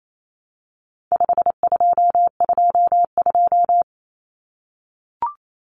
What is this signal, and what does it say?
Morse code, heard as a keyed tone, about 28 wpm, 700 Hz. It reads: 5222
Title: Numbers - 4-Digit 28wpm